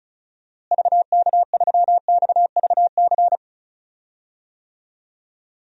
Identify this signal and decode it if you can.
Morse code, heard as a keyed tone, about 34 wpm, 700 Hz. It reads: VK3XVC
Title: DXCC-100 Call Signs - 6 Characters - Repeated 3x - 35wpm